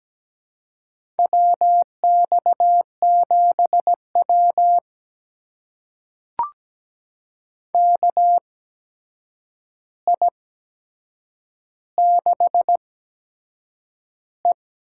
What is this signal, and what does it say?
Morse code, heard as a keyed tone, about 17 wpm, spaced as for 7 wpm, 700 Hz. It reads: WX7W KI6E
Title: USA Call Signs - 4 Characters - 8x Character Spacing - 17wpm